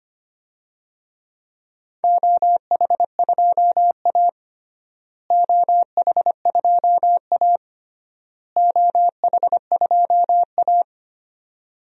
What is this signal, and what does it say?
Morse code, heard as a keyed tone, about 25 wpm, 700 Hz. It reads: OH2A OH2A OH2A
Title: DXCC-100 Call Signs - 4 Characters - Repeated 3x - 25wpm